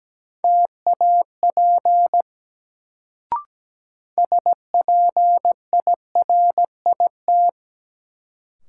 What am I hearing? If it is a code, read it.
Morse code, heard as a keyed tone, about 17 wpm, 700 Hz. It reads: TAP SPIRIT
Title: Sentences from Top 1400 Words - New Word Drill Alt - 17wpm